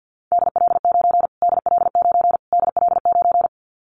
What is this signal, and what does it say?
Morse code, heard as a keyed tone, about 50 wpm, 700 Hz. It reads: DL9 DL9 DL9
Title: Callsign Prefixes for Top 25 DXCC - Repeated 3x - 50wpm